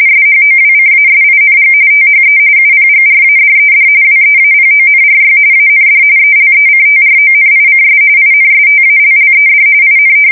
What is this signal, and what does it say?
RTTY, about 45.45 baud. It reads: SHOWCASING THE UPGRADED WEAPONRY AND ELECTRONICS INSTALLED IN
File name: fsk170 / USS-New-Jersey